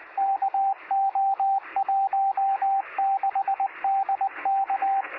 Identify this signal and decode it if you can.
Morse code, heard as a keyed tone, about 24 wpm, 780 Hz, with radio noise. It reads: KO16DK
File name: sm7lcb / 0319z yl2ao 23cm